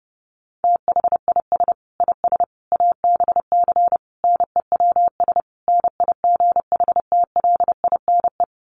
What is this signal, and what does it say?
Morse code, heard as a keyed tone, about 30 wpm, 700 Hz. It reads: THIS IS ABC NEWS NIGHTLINE
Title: Sets of 5 Words 30wpm